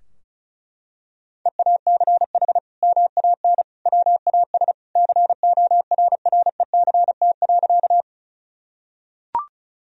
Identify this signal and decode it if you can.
Morse code, heard as a keyed tone, about 34 wpm, 700 Hz. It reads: EACH MAN WAS CORRECT.
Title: Sentences from Top 1000 Words - Review Part 2 - 35wpm